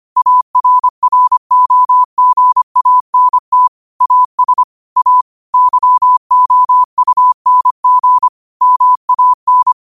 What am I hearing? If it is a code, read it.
Morse code, heard as a keyed tone, about 25 wpm, 1000 Hz. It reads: ARROGANT AS A YOUNG MAN
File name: qotdfiles / 25wpm